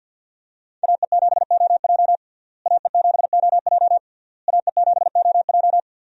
Morse code, heard as a keyed tone, about 50 wpm, 700 Hz. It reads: AE7OJ AE7OJ AE7OJ
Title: USA Call Signs - 5 Characters - Repeated 3x - 50wpm